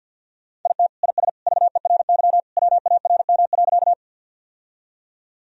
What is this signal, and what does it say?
Morse code, heard as a keyed tone, about 50 wpm, 700 Hz. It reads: IT IS VERY WARM.
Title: Sentences from Top 1000 Words - Review Part 2 - 50wpm